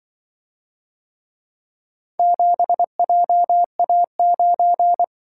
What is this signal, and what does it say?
Morse code, heard as a keyed tone, about 24 wpm, 700 Hz. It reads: 7JA9
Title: Speed-Racing - Four Letter-Number Combinations - 20wpm